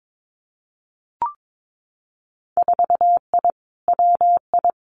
Morse code, heard as a keyed tone, about 22 wpm, 700 Hz. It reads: 4I WI
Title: Winter Field Day Exchanges 22wpm